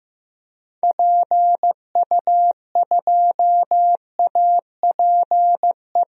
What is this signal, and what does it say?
Morse code, heard as a keyed tone, about 15 wpm, 700 Hz. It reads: PU2APE
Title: DXCC-100 Call Signs - 6 Characters - Repeated 3x - 15wpm